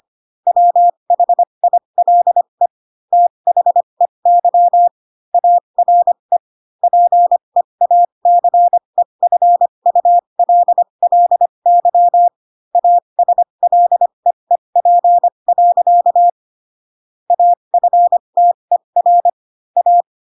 Morse code, heard as a keyed tone, about 25 wpm, spaced as for 22 wpm, 700 Hz. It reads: WHILE THEY ARE PEACEFULLY ASLEEP. AFTER A
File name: Pinocchio / Pinocchio ch6 18 25 ik1hge